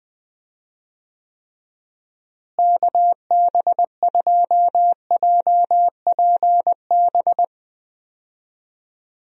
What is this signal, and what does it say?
Morse code, heard as a keyed tone, about 20 wpm, 700 Hz. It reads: KB2JPB
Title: USA Call Signs - 6 Characters - Repeated 3x - 20wpm